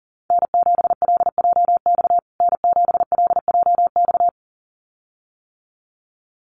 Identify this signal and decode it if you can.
Morse code, heard as a keyed tone, about 40 wpm, 700 Hz. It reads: N7LJX N7LJX
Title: USA Call Signs Repeated 2x 40wpm